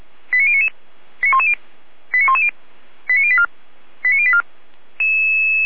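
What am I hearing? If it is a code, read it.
Five-tone selective call (ZVEI), heard as quick runs of five tones, tones of about 70 ms. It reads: 89099 89110 89084 89084
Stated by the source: ZVEI five-tone selective call received on SDR, address 89097.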